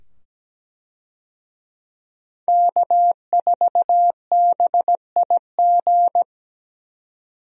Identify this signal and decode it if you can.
Morse code, heard as a keyed tone, about 17 wpm, 700 Hz. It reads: K4BIG
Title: USA Call Signs - 5 Characters - 17wpm